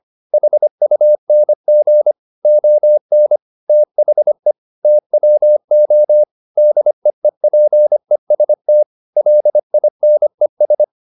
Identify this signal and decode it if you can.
Morse code, heard as a keyed tone, about 25 wpm, 600 Hz. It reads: HUNG ON THE TWO DEEPEST LINES
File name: the old man and the sea part 1 25wpm